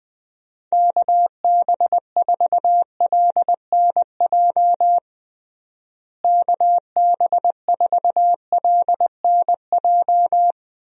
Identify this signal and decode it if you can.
Morse code, heard as a keyed tone, about 20 wpm, 700 Hz. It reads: KB4LNJ KB4LNJ
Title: USA Call Signs - 6 Characters - Repeated 3x - 20wpm